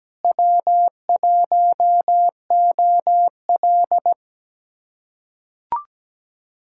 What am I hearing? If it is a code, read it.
Morse code, heard as a keyed tone, about 17 wpm, 700 Hz. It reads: W1OL
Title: USA Callsign DE Callsign - 4 Characters - 17wpm